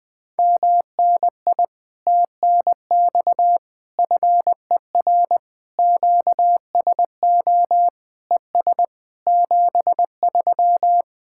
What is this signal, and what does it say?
Morse code, heard as a keyed tone, about 20 wpm, 700 Hz. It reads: MNI TNX FER QSO ES 73
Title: Speed-Racing Common QSO Phrases - 20wpm